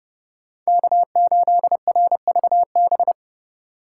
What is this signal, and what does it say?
Morse code, heard as a keyed tone, about 33 wpm, 700 Hz. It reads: K8RVB